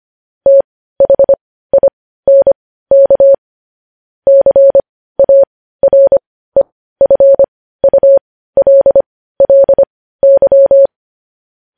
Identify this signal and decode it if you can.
Morse code, heard as a keyed tone, about 25 wpm, spaced as for 15 wpm, 550 Hz. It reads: THINK CAREFULLY
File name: podcast / News headlines-1x-25x15